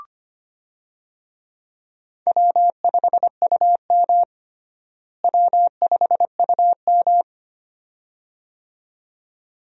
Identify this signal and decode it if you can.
Morse code, heard as a keyed tone, about 25 wpm, 700 Hz. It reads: W5UM W5UM
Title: USA Call Signs - 4 Characters - Repeated 2x - 25wpm